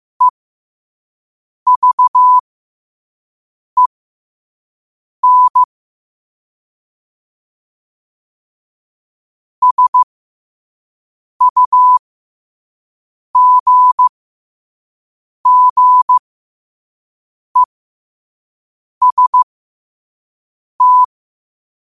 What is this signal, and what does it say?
Morse code, heard as a keyed tone, about 15 wpm, spaced as for 5 wpm, 1000 Hz. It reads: EVEN SUGGEST
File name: qotdfiles / 5wpm